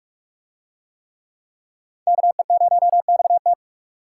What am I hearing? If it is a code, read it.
Morse code, heard as a keyed tone, about 44 wpm, 700 Hz. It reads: KE0XT